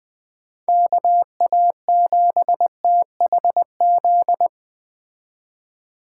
Morse code, heard as a keyed tone, about 20 wpm, 700 Hz. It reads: KA7THZ
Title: Speed Racing - USA Call Signs 17wpm